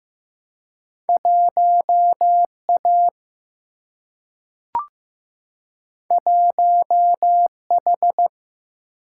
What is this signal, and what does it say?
Morse code, heard as a keyed tone, about 15 wpm, 700 Hz. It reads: 1 A 1 H
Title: Single-Letter Single-Number 15wpm